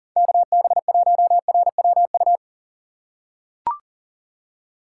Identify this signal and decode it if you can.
Morse code, heard as a keyed tone, about 40 wpm, 700 Hz. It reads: KB1RWU
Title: USA Call Signs - 6 Characters - 4x Character Spacing - 40wpm